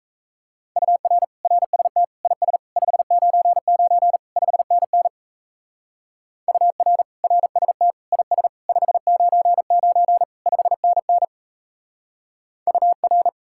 Morse code, heard as a keyed tone, about 34 wpm, 700 Hz. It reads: UR RST IS 599 5NN UR RST IS 599 5NN UR
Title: Speed-Racing Common QSO Phrases Encore - 28wpm